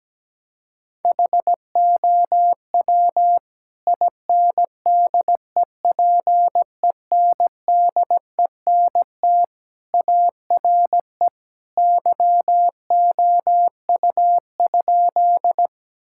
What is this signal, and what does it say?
Morse code, heard as a keyed tone, about 17 wpm, 700 Hz. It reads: HOW INDEPENDENT ARE YOU?
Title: Sentences from Top 1000 Words - Review Part 6 - 17wpm